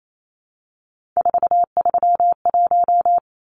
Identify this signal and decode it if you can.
Morse code, heard as a keyed tone, about 28 wpm, 700 Hz. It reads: 431
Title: Numbers - 3-Digit 28wpm